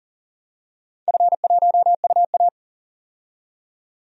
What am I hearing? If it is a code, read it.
Morse code, heard as a keyed tone, about 40 wpm, 700 Hz. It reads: F1UA
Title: DXCC-100 Call Signs - 4 Characters - 40wpm